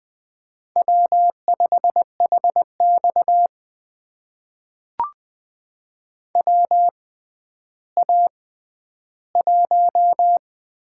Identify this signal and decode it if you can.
Morse code, heard as a keyed tone, about 20 wpm, spaced as for 11 wpm, 700 Hz. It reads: W5HX WA1
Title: USA Call Signs - 4 Characters - 6x Character Spacing - 20wpm